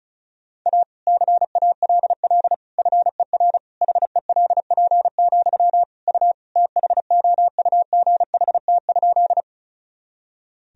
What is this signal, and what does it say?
Morse code, heard as a keyed tone, about 36 wpm, 700 Hz. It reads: A CALL FER HELP, U THOUGHT?
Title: Sentences from Top 300 Words and Abbreviations 35wpm